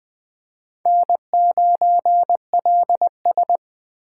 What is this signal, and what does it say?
Morse code, heard as a keyed tone, about 20 wpm, 700 Hz. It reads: N9LS
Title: USA Call Signs - 4 Characters - 4x Character Spacing - 20wpm